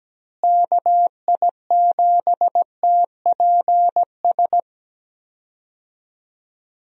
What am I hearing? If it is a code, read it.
Morse code, heard as a keyed tone, about 17 wpm, 700 Hz. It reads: KI7TPS
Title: USA Call Signs - 6 Characters - Repeated 2x - 17wpm